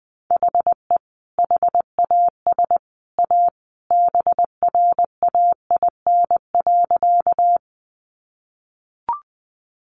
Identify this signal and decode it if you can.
Morse code, heard as a keyed tone, about 20 wpm, 700 Hz. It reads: HE HAS A BRAIN.
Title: Sentences from Top 1100 Words 20wpm